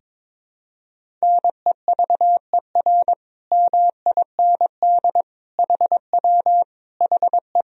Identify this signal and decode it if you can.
Morse code, heard as a keyed tone, about 21 wpm, 700 Hz. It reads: NEVER MIND HW HE
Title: Sentences from Top 500 Words and Abbreviations 22wpm